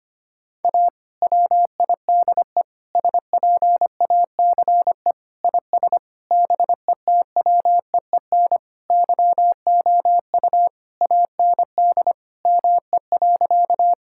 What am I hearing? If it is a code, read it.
Morse code, heard as a keyed tone, about 25 wpm, 700 Hz. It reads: A WIDE SPACE IS BETWEEN YOU AND ME.
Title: Sentences from Top 1000 Words - Review Part 6 - 25wpm